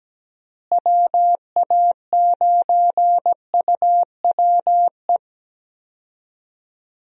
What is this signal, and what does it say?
Morse code, heard as a keyed tone, about 17 wpm, 700 Hz. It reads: WA9UWE
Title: USA Call Signs - 6 Characters - Repeated 3x - 17wpm